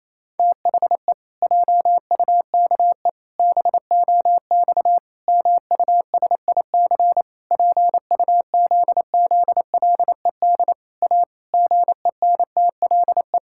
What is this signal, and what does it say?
Morse code, heard as a keyed tone, about 28 wpm, 700 Hz. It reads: THE JUKE BOX MUSIC PUZZLED A GENTLE
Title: Pangram Challenge v2 - 28wpm